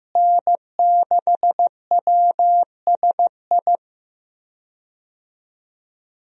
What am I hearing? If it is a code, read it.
Morse code, heard as a keyed tone, about 15 wpm, 700 Hz. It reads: N6WSI
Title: USA Call Signs - 5 Characters - Repeated 2x - 15wpm